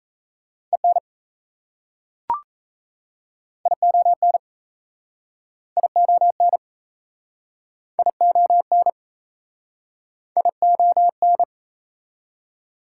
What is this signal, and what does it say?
Morse code, heard as a keyed tone, about 38 wpm, 700 Hz. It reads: EN ION ION ION ION